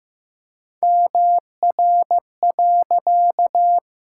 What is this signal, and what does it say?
Morse code, heard as a keyed tone, about 16 wpm, 700 Hz. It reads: MR.
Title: Pangram Challenge v2 - 15wpm